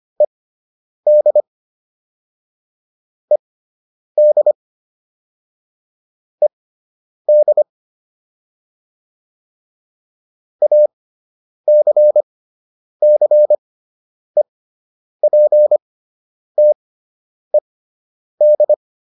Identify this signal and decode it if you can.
Morse code, heard as a keyed tone, about 25 wpm, spaced as for 9 wpm, 600 Hz. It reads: ED ED ED ACCEPTED